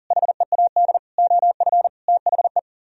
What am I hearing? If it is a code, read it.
Morse code, heard as a keyed tone, about 52 wpm, 700 Hz. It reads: HEAD OF THE